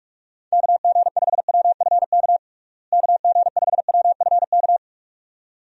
Morse code, heard as a keyed tone, about 42 wpm, 700 Hz. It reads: KG5WFK KG5WFK
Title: USA Call Signs - 6 Characters - Repeated 2x - 45wpm